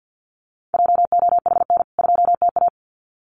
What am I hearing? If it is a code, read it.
Morse code, heard as a keyed tone, about 48 wpm, 700 Hz. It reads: JOHN 2TA
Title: CWT Exchange 50wpm